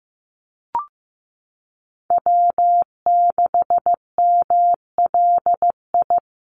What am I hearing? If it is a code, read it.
Morse code, heard as a keyed tone, about 15 wpm, 700 Hz. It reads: W6MLI
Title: USA Call Signs Repeated 2x 15wpm